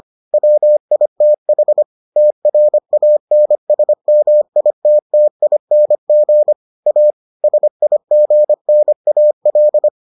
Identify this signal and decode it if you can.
Morse code, heard as a keyed tone, about 25 wpm, 600 Hz. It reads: WITH TRANSMITTING A SIGNAL